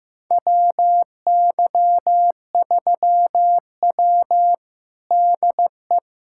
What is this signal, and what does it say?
Morse code, heard as a keyed tone, about 15 wpm, 700 Hz. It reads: WY3W DE
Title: USA Callsign DE Callsign - 4 Characters - 15wpm